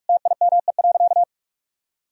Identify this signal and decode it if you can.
Morse code, heard as a keyed tone, about 46 wpm, 700 Hz. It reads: TIME.
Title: Sentences from Top 1000 Words - Review Part 3 - 45wpm